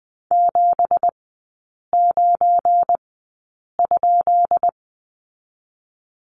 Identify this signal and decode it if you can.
Morse code, heard as a keyed tone, about 20 wpm, spaced as for 8 wpm, 690 Hz. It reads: 79?
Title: 205 Lesson 33e - Letter j - Three Characters 20wpm